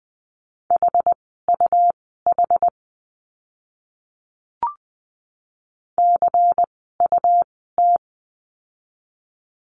Morse code, heard as a keyed tone, about 20 wpm, spaced as for 14 wpm, 700 Hz. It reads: HUH CUT
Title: 101 Lesson 17f - Letter u - Words 20wpm